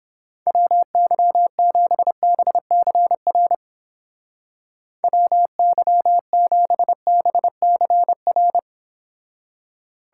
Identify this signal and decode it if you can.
Morse code, heard as a keyed tone, about 31 wpm, 700 Hz. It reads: WY7BCR WY7BCR